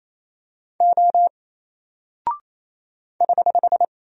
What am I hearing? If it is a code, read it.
Morse code, heard as a keyed tone, about 27 wpm, 700 Hz. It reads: O <HH>
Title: Single Character (Letter:Number:Punctuation:Prosign) 28wpm